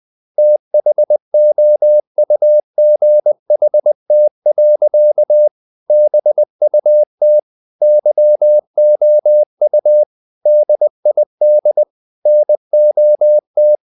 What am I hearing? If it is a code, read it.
Morse code, heard as a keyed tone, about 20 wpm, 600 Hz. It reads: THOUGHT. BUT YOU DID NOT
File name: the old man and the sea part 2 20wpm